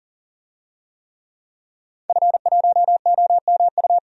Morse code, heard as a keyed tone, about 40 wpm, 700 Hz. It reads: F1OMU
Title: DXCC-100 Call Signs - 5 Characters - 40wpm